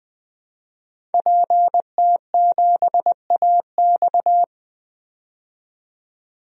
Morse code, heard as a keyed tone, about 20 wpm, 700 Hz. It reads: PT7AX
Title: DXCC-100 Call Signs - 5 Characters - 20wpm